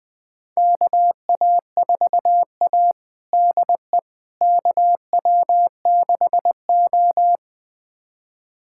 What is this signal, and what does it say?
Morse code, heard as a keyed tone, about 20 wpm, 700 Hz. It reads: KA4A DE KW6O
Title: USA Callsign DE Callsign - 4 Characters - 20wpm